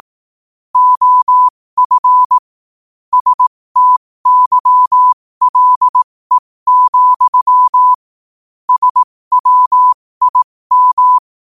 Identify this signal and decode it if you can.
Morse code, heard as a keyed tone, about 18 wpm, spaced as for 15 wpm, 1000 Hz. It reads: OF STYLE, SWIM
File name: 15wpm